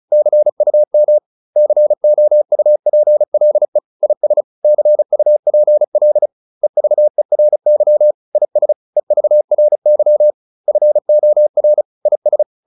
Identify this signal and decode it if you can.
Morse code, heard as a keyed tone, about 34 wpm, 600 Hz. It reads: CUM COUPLE IS CUPL EVERY IS EVRY FOR IS